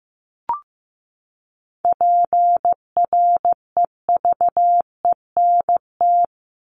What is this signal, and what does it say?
Morse code, heard as a keyed tone, about 15 wpm, 700 Hz. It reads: PREVENT
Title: Top 1001-1250 Words 15wpm